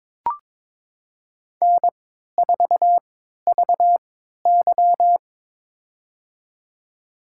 Morse code, heard as a keyed tone, about 22 wpm, spaced as for 13 wpm, 700 Hz. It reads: N4VY
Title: USA Call Signs - 4 Characters - 3x Character Spacing - 22wpm